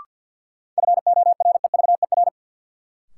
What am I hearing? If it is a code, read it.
Morse code, heard as a keyed tone, about 52 wpm, 700 Hz. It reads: FOREVER
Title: Sentences from Top 1600 Words - New Word Drill - 50wpm